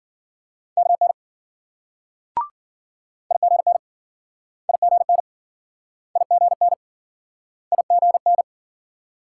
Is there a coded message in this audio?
Yes: Morse code from 0.8 to 8.4 s, heard as a keyed tone, about 53 wpm, 700 Hz. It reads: KN IGN IGN IGN IGN